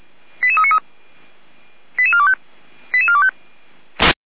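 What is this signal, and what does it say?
Five-tone selective call (ZVEI), heard as quick runs of five tones, tones of about 80 ms. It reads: 88393 88425 88425
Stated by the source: Five-tone ZVEI selective call, received on an SDR.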